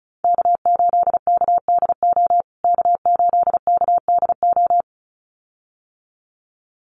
Morse code, heard as a keyed tone, about 34 wpm, 700 Hz. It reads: K8KDO K8KDO
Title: USA Call Signs Repeated 2x 35wpm